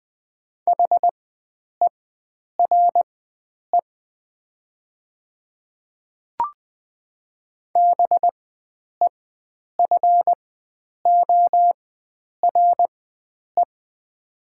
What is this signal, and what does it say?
Morse code, heard as a keyed tone, about 20 wpm, spaced as for 9 wpm, 700 Hz. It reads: HERE BEFORE
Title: 4x Character Spacing - Top 300 Words - 20wpm